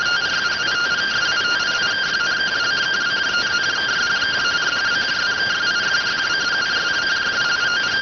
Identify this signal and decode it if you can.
Two-tone teleprinter signal, about 170 Hz shift.